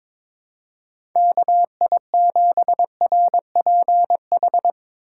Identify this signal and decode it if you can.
Morse code, heard as a keyed tone, about 21 wpm, 700 Hz. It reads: KI7RPH
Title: USA Call Signs - 6 Characters - 2x Character Spacing - 22wpm